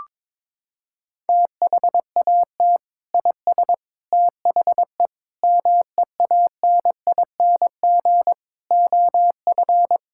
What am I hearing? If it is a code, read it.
Morse code, heard as a keyed tone, about 22 wpm, 700 Hz. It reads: THAT IS THE MEANING OF
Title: Sentences from Top 1000 Words - Review Part 6 - 22wpm